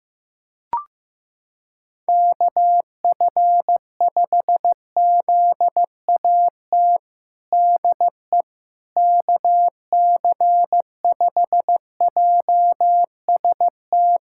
Morse code, heard as a keyed tone, about 15 wpm, 700 Hz. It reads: KF5ZAT DE KC5JST
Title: USA Callsign DE Callsign - 6 Characters - 15wpm